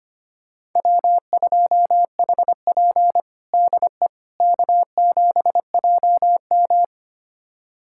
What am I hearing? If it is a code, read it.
Morse code, heard as a keyed tone, about 25 wpm, 700 Hz. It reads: W2HP DE K7JM
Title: USA Callsign DE Callsign - 4 Characters - 25wpm